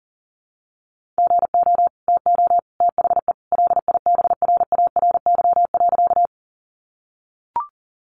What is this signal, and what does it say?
Morse code, heard as a keyed tone, about 38 wpm, 700 Hz. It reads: GO TO THE LIBRARY.
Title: Sentences from Top 600 Words 40wpm